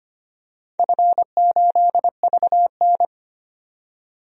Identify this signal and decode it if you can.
Morse code, heard as a keyed tone, about 25 wpm, 700 Hz. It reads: F8VN
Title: DXCC-100 Call Signs - 4 Characters - 25wpm